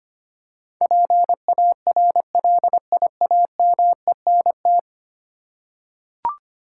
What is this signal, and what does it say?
Morse code, heard as a keyed tone, about 25 wpm, 700 Hz. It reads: PARLIAMENT
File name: Ten Letter Words - 25wpm